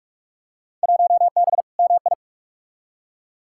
Morse code, heard as a keyed tone, about 45 wpm, 700 Hz. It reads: 1B MI
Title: Field Day Exchanges - 45wpm